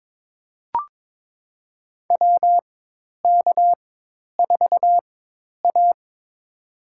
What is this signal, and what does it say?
Morse code, heard as a keyed tone, about 22 wpm, spaced as for 10 wpm, 700 Hz. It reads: WK4A
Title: USA Call Signs - 4 Characters - 4x Character Spacing - 22wpm